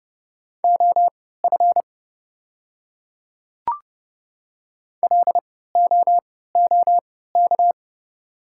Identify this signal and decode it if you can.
Morse code, heard as a keyed tone, about 29 wpm, spaced as for 17 wpm, 700 Hz. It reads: OF LOOK
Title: 3x Character Spacing - Top 200 Words - 30wpm